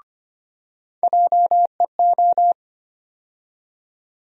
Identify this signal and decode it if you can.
Morse code, heard as a keyed tone, about 25 wpm, 700 Hz. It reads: JEO
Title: USA Call Sign Postfixes - 3 Letters 25wpm